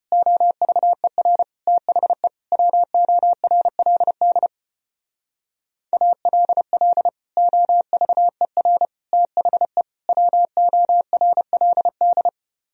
Morse code, heard as a keyed tone, about 33 wpm, 700 Hz. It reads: OVER THE WORLD ALL OVER THE WORLD